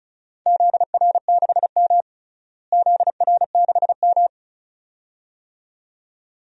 Morse code, heard as a keyed tone, about 36 wpm, 700 Hz. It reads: ZR6M ZR6M
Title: DXCC-100 Call Signs - 4 Characters - Repeated 2x - 35wpm